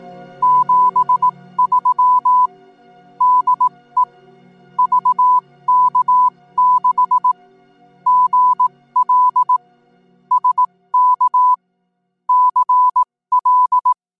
Morse code, heard as a keyed tone, about 18 wpm, spaced as for 14 wpm, 1000 Hz. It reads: 73 DE VK6GL SK CL